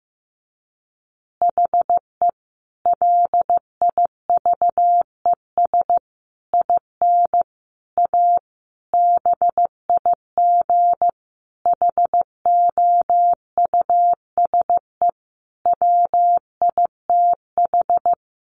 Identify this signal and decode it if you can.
Morse code, heard as a keyed tone, about 15 wpm, 700 Hz. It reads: HE LIVES IN A BIG HOUSE WITH
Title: Additional Easy Phrases 15wpm